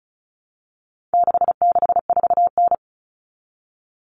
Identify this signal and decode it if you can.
Morse code, heard as a keyed tone, about 34 wpm, 700 Hz. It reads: 664N
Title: Common Cut Numbers - 4-Digit 35wpm